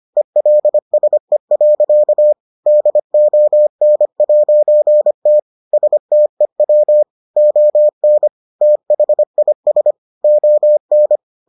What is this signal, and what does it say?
Morse code, heard as a keyed tone, about 25 wpm, 600 Hz. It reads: ELSE. DON'T STEW ON THIS ON